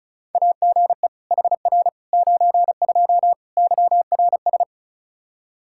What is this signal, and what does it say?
Morse code, heard as a keyed tone, about 36 wpm, 700 Hz. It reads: AGE HR 92 YRS
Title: Practice QSO - Exchange 3 - More Info - 35wpm